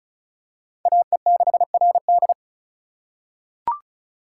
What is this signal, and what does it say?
Morse code, heard as a keyed tone, about 34 wpm, 700 Hz. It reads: AE6RD
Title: USA Call Signs - 5 Characters - 4x Character Spacing - 35wpm